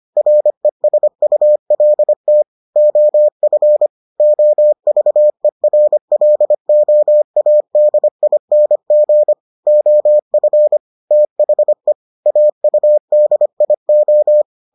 Morse code, heard as a keyed tone, about 25 wpm, 600 Hz. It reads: RESULT OF OVERLOADING OF THE AUDIO